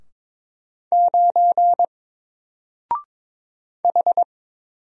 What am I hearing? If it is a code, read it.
Morse code, heard as a keyed tone, about 22 wpm, 700 Hz. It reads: 9 H
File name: Single-Letter-Number-22wpm